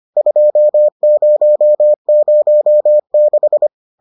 Morse code, heard as a keyed tone, about 25 wpm, 600 Hz. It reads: 2006